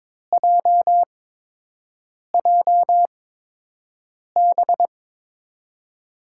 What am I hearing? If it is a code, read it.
Morse code, heard as a keyed tone, about 22 wpm, spaced as for 6 wpm, 700 Hz. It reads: JJB
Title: USA Call Signs - 6 Characters - 8x Character Spacing - 22wpm